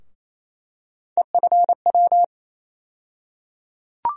Morse code, heard as a keyed tone, about 28 wpm, 700 Hz. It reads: EFW
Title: USA Call Sign Postfixes - 3 Letters 28wpm